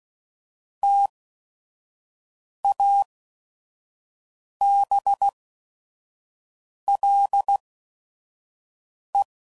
Morse code, heard as a keyed tone, about 16 wpm, spaced as for 5 wpm, 800 Hz. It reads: TABLE
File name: CW Code Groups @5WPM / Technical Subjects / AND Gate